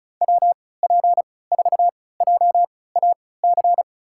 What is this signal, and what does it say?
Morse code, heard as a keyed tone, about 34 wpm, spaced as for 20 wpm, 700 Hz. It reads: WP4JAC
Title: USA Call Signs - 6 Characters - 3x Character Spacing - 35wpm